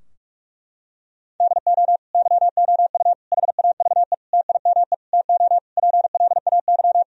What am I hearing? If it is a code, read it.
Morse code, heard as a keyed tone, about 46 wpm, 700 Hz. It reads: DO YOU HAVE TIME TO PLAY
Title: Sentences from Top 1000 Words - Review Part 4 - 45wpm